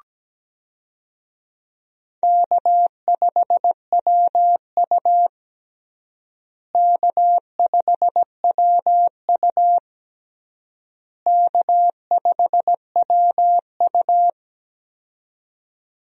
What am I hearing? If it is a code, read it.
Morse code, heard as a keyed tone, about 17 wpm, 700 Hz. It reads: K5WU K5WU K5WU
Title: USA Call Signs - 4 Characters - Repeated 3x - 17wpm